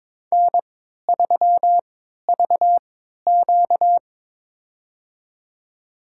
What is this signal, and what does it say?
Morse code, heard as a keyed tone, about 21 wpm, spaced as for 12 wpm, 700 Hz. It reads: N3VQ
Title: USA Call Signs - 4 Characters - 3x Character Spacing - 22wpm